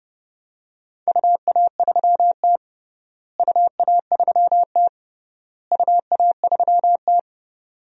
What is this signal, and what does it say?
Morse code, heard as a keyed tone, about 30 wpm, 700 Hz. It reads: UA3T UA3T UA3T
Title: DXCC-100 Call Signs - 4 Characters - Repeated 3x - 30wpm